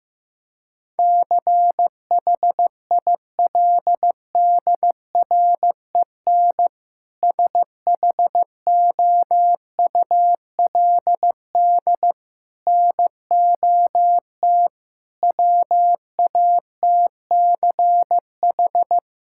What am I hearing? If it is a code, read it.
Morse code, heard as a keyed tone, about 15 wpm, 700 Hz. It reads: CHILDREN SHOULD NOT WATCH
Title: Sentences from Top 1000 Words - Review Part 5 - 15wpm